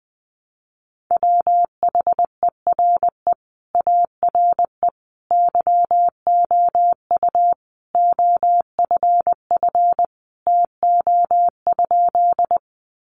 Morse code, heard as a keyed tone, about 20 wpm, 700 Hz. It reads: WHERE ARE YOU OFF TO?
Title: Sentences from Top 200 Words 20wpm